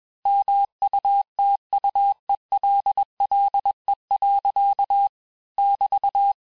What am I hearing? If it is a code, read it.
Morse code, heard as a keyed tone, about 21 wpm, 800 Hz. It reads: MUTUELLE. <BT>
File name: texte-clair-vitesse-15mpm-puis-20mpm